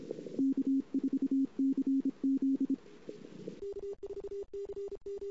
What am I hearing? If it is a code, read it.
Morse code, heard as a keyed tone, about 26 wpm, 270 Hz. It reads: K4CZ
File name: CW / 2008-ss-cw-k5tr-op-n5rz -23:00